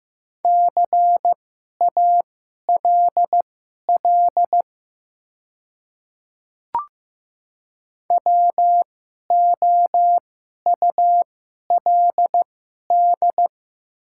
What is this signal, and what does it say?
Morse code, heard as a keyed tone, about 15 wpm, spaced as for 11 wpm, 690 Hz. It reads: CALL WOULD
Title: 2x Character Spacing - Top 200 Words - 15wpm